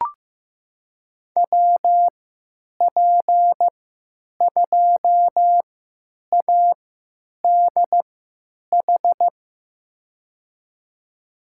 Morse code, heard as a keyed tone, about 15 wpm, spaced as for 9 wpm, 700 Hz. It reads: WP2ADH
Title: USA Call Signs - 6 Characters - 3x Character Spacing - 15wpm